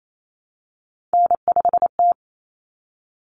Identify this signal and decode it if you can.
Morse code, heard as a keyed tone, about 28 wpm, 700 Hz. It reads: N5T
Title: Common Cut Numbers - 3-Digit 28wpm